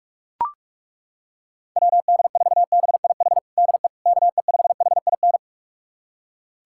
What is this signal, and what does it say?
Morse code, heard as a keyed tone, about 41 wpm, 700 Hz. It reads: WD4BIH DE KE5HIN
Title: USA Callsign DE Callsign 45wpm